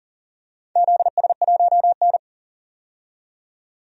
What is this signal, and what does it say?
Morse code, heard as a keyed tone, about 40 wpm, 700 Hz. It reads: ZS1N
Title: DXCC-100 Call Signs - 4 Characters - 40wpm